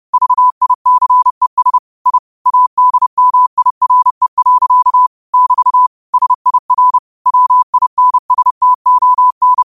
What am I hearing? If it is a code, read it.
Morse code, heard as a keyed tone, about 30 wpm, 1000 Hz. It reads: VICES I ADMIRE. <BT> SIR WINSTON
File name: qotdfiles / 30wpm